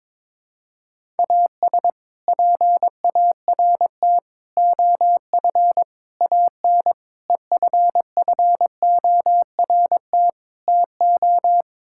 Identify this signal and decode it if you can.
Morse code, heard as a keyed tone, about 22 wpm, 700 Hz. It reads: AS PART OF AN EFFORT TO
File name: practice / 22wpm / Sets of 6 Words - 22wpm